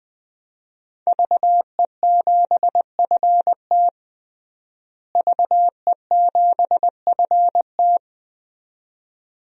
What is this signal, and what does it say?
Morse code, heard as a keyed tone, about 20 wpm, 700 Hz. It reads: VE7FT VE7FT
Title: DXCC-100 Call Signs - 5 Characters - Repeated 2x - 20wpm